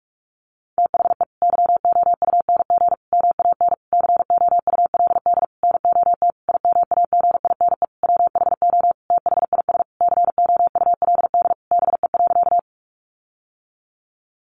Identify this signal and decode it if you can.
Morse code, heard as a keyed tone, about 44 wpm, 700 Hz. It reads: THE YOUNG MAN COULD NOT IMAGINE WHO THIS COULD BE.
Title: Sentences from Top 1400 Words 45wpm